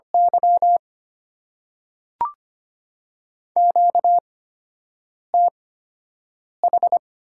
Morse code, heard as a keyed tone, about 25 wpm, spaced as for 7 wpm, 700 Hz. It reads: Y QTH
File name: Top 90 QSO Elements - 8x Character Spacing - 25wpm